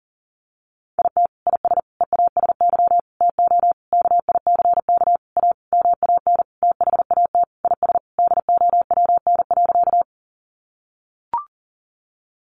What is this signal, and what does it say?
Morse code, heard as a keyed tone, about 40 wpm, 700 Hz. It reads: IT IS EASY TO KICK A MAN THAT IS DOWN.
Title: Sentences from Top 1300 Words 40wpm